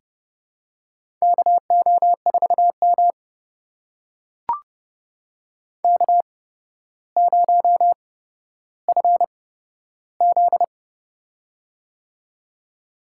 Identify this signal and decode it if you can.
Morse code, heard as a keyed tone, about 31 wpm, spaced as for 13 wpm, 700 Hz. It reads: KO4M K0FZ
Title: USA Call Signs - 4 Characters - 8x Character Spacing - 30wpm